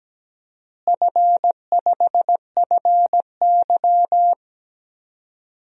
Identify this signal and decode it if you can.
Morse code, heard as a keyed tone, about 17 wpm, 700 Hz. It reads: F5FY
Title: DXCC-100 Call Signs - 4 Characters - Repeated 2x - 17wpm